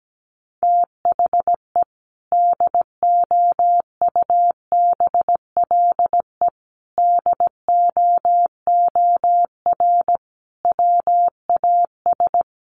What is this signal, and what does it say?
Morse code, heard as a keyed tone, about 17 wpm, 700 Hz. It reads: THE DOUBLE DOOR WAS
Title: Sentences from Top 1400 Words 17wpm